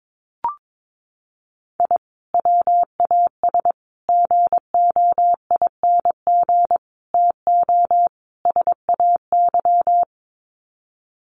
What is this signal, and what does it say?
Morse code, heard as a keyed tone, about 22 wpm, 700 Hz. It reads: I WAS GOING TO SAY
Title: Sets of 5 Words 22wpm